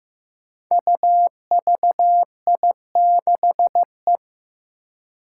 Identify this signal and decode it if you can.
Morse code, heard as a keyed tone, about 15 wpm, 700 Hz. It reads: UVI6E
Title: Code Groups - 5 Letter:Numbers - 15wpm